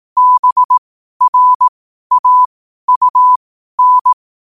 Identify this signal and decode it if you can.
Morse code, heard as a keyed tone, about 18 wpm, spaced as for 12 wpm, 1000 Hz. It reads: BRAUN
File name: qotdfiles / 12wpm